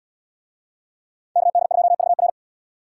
Morse code, heard as a keyed tone, about 76 wpm, 690 Hz. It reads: KD2VR